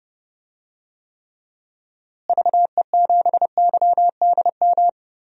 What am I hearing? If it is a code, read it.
Morse code, heard as a keyed tone, about 30 wpm, 700 Hz. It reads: VE7YDM
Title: DXCC-100 Call Signs - 6 Characters - Repeated 2x - 30wpm